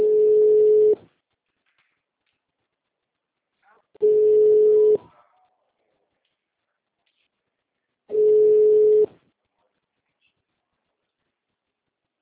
Ringback tone (425 Hz, 1 s on, 3 s off).